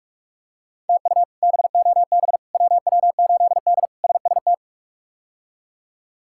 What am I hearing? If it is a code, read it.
Morse code, heard as a keyed tone, about 44 wpm, 700 Hz. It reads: TU BOB WW8D SST
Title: K1USN SST QSO Practice - 45wpm